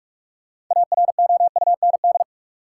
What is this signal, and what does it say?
Morse code, heard as a keyed tone, about 41 wpm, 700 Hz. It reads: AROUND
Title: Sets of 2 Words - New Word Drill - 45wpm